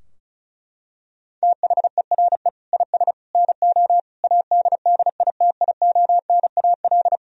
Morse code, heard as a keyed tone, about 35 wpm, 700 Hz. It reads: THERE IS NO ADDITIONAL
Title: Sentences from Top 1000 Words - Review Part 4 - 35wpm